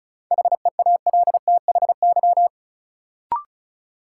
Morse code, heard as a keyed tone, about 34 wpm, 700 Hz. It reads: HEALTHY
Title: Sentences from Top 1400 Words - New Word Drill Alt - 35wpm